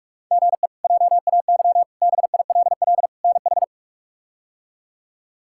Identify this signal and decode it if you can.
Morse code, heard as a keyed tone, about 46 wpm, 700 Hz. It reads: GE JAY BILL NH
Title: K1USN SST QSO Practice - 45wpm